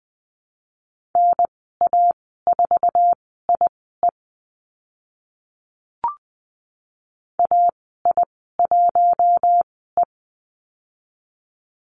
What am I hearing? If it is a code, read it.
Morse code, heard as a keyed tone, about 20 wpm, spaced as for 15 wpm, 700 Hz. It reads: NA4IE AI1E
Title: 062 Lesson 11g - Letter h - Call Signs 20wpm